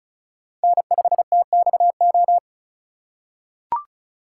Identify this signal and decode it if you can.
Morse code, heard as a keyed tone, about 34 wpm, 700 Hz. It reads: N5TXO
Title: USA Call Signs - 5 Characters - 3x Character Spacing - 35wpm